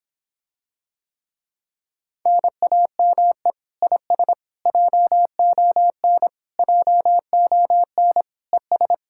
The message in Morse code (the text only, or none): NAME IS JON JON ES